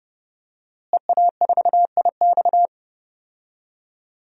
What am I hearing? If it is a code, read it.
Morse code, heard as a keyed tone, about 32 wpm, 700 Hz. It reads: EA4IX
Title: DXCC-100 Call Signs - 5 Characters - Repeated 3x - 30wpm